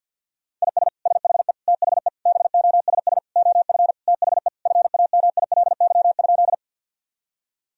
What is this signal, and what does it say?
Morse code, heard as a keyed tone, about 50 wpm, 700 Hz. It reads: IS SHE THE BOSS OF THE FAMILY?
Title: Sentences from Top 1000 Words - Review Part 4 - 50wpm